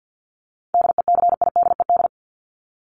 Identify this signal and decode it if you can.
Morse code, heard as a keyed tone, about 50 wpm, 700 Hz. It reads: DECIDED
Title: ED Suffix - Top 100 Words with Base Words 50wpm